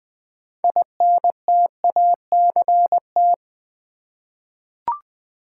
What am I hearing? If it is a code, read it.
Morse code, heard as a keyed tone, about 20 wpm, 700 Hz. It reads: INTACT
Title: 6 Letter Words - First 3 Letters Given - 20wpm